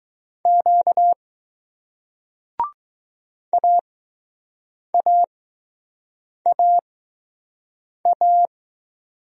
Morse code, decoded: Q AAAA

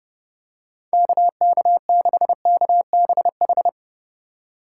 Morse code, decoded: KK6KBH

